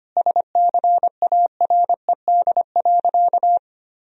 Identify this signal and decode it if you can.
Morse code, heard as a keyed tone, about 25 wpm, 700 Hz. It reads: SCARED.
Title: Sentences from Top 1000 Words - Review Part 4 - 25wpm